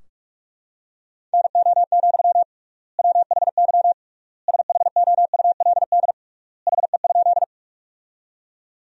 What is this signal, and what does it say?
Morse code, heard as a keyed tone, about 46 wpm, 700 Hz. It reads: NO, WHY SHOULD HE?
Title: 3x Word-Spacing - Sentences from Top 300 Words - 45wpm